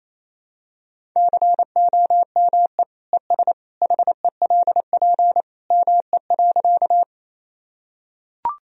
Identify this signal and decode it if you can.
Morse code, heard as a keyed tone, about 28 wpm, 700 Hz. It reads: COME ES HELP ME.
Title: Sentences from Top 200 Words and Abbreviations 28wpm